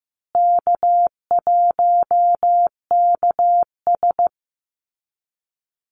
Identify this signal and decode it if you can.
Morse code, heard as a keyed tone, about 15 wpm, 700 Hz. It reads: K1KS
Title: Easy Call Signs Repeated 3x - 15wpm